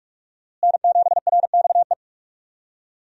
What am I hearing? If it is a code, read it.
Morse code, heard as a keyed tone, about 46 wpm, 700 Hz. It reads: N7RXE
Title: USA Call Signs - 5 Characters - Repeated 2x - 45wpm